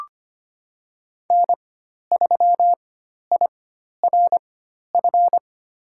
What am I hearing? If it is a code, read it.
Morse code, heard as a keyed tone, about 25 wpm, spaced as for 12 wpm, 700 Hz. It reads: N3IRF
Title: USA Call Signs - 5 Characters - 4x Character Spacing - 25wpm